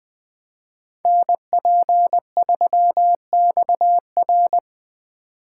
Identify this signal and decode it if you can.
Morse code, heard as a keyed tone, about 24 wpm, 700 Hz. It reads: NP3XR
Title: Speed Racing - USA Call Signs 20wpm